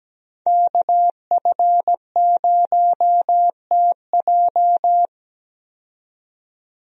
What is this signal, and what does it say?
Morse code, heard as a keyed tone, about 17 wpm, 700 Hz. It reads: KF0TJ
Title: USA Call Signs - 5 Characters - 17wpm